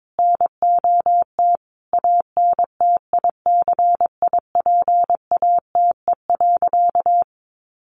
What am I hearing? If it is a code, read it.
Morse code, heard as a keyed tone, about 22 wpm, 700 Hz. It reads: NOT ANTICIPATE.
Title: Sentences from Top 1500 Words 22wpm